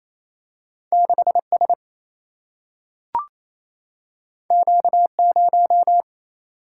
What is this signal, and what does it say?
Morse code, heard as a keyed tone, about 28 wpm, 700 Hz. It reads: 6 S Q 0
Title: Single-Letter Single-Number 28wpm